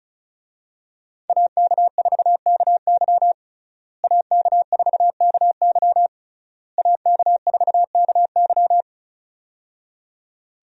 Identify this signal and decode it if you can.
Morse code, heard as a keyed tone, about 36 wpm, 700 Hz. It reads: AK4KY AK4KY AK4KY
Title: USA Call Signs - 5 Characters - Repeated 3x - 35wpm